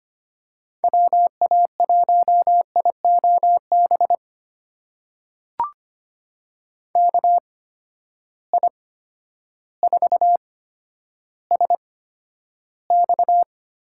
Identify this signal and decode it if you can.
Morse code, heard as a keyed tone, about 25 wpm, spaced as for 11 wpm, 700 Hz. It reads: WA1IOB KI4SX
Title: USA Call Signs - 6 Characters - 8x Character Spacing - 25wpm